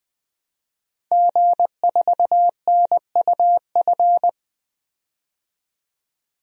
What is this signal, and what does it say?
Morse code, heard as a keyed tone, about 20 wpm, 700 Hz. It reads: G4NUF
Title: DXCC-100 Call Signs - 5 Characters - 20wpm